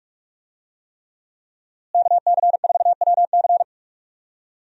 Morse code, heard as a keyed tone, about 41 wpm, 700 Hz. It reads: KC4WC
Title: USA Call Signs - 5 Characters - Repeated 3x - 45wpm